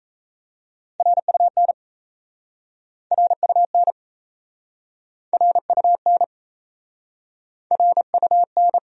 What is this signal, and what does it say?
Morse code, heard as a keyed tone, about 37 wpm, 700 Hz. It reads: RUN RUN RUN RUN